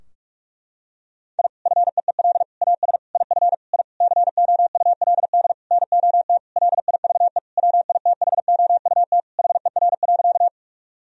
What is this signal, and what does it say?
Morse code, heard as a keyed tone, about 41 wpm, 700 Hz. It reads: I FEEL AS IF I COULD NOT LIVE WITHOUT HER.
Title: Sentences from Top 1000 Words - Review Part 3 - 45wpm